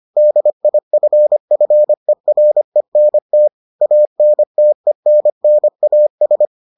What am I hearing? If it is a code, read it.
Morse code, heard as a keyed tone, about 25 wpm, 600 Hz. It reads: DIFFERENT ANTENNAS